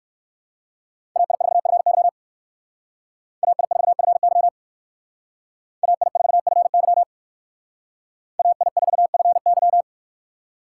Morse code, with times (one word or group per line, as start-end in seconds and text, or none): 1.2-2.1 AI4FY
3.4-4.5 AI4FY
5.8-7.0 AI4FY
8.4-9.8 AI4FY